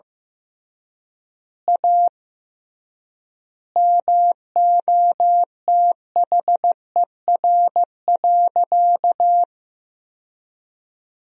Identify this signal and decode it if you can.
Morse code, heard as a keyed tone, about 15 wpm, 700 Hz. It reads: A MOTHER.
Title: 3x Word-Spacing - Sentences from Top 200 Words - 15wpm